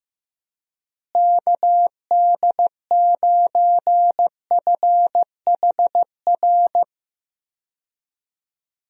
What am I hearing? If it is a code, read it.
Morse code, heard as a keyed tone, about 16 wpm, 700 Hz. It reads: KD9FHR
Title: USA Call Signs - 6 Characters - Repeated 2x - 15wpm